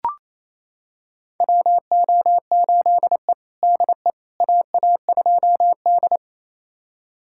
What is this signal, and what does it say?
Morse code, heard as a keyed tone, about 28 wpm, 700 Hz. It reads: WO8E DE AA2D
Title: USA Callsign DE Callsign - 4 Characters - 28wpm